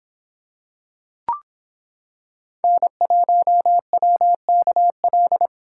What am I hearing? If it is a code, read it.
Morse code, heard as a keyed tone, about 25 wpm, 700 Hz. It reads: N1WKL